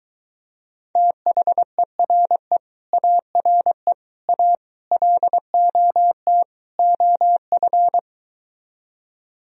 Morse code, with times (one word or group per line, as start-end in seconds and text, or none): 1.0-2.6 THERE
2.9-3.9 ARE
4.3-4.5 A
4.9-6.4 LOT
6.8-8.0 OF